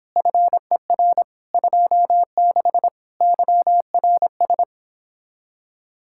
Morse code, heard as a keyed tone, about 29 wpm, 700 Hz. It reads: FER 26 YRS